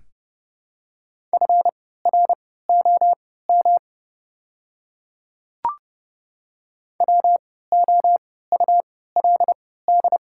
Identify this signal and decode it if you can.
Morse code, heard as a keyed tone, about 30 wpm, spaced as for 17 wpm, 690 Hz. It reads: FROM WOULD
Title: 3x Character Spacing - Top 200 Words - 30wpm